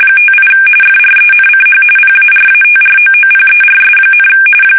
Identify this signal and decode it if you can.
RTTY, about 45.45 baud. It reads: 58 FRENCH PARATROOPERS,
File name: fsk850 / battleshipnj